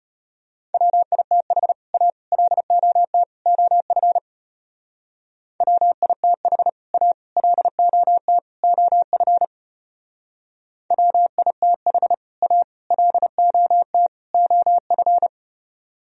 Morse code, heard as a keyed tone, about 33 wpm, 700 Hz. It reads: WITH A LOT OF WITH A LOT OF WITH A LOT OF